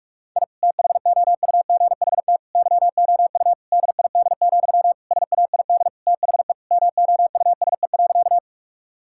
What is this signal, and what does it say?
Morse code, heard as a keyed tone, about 46 wpm, 700 Hz. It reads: I THOUGHT YOU DID, SAID THE MOUSE.
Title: Sentences from Top 1000 Words - Review Part 5 - 45wpm